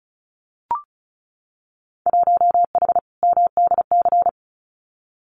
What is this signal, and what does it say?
Morse code, heard as a keyed tone, about 36 wpm, 690 Hz. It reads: 1H MDC
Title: Winter Field Day Exchanges 35wpm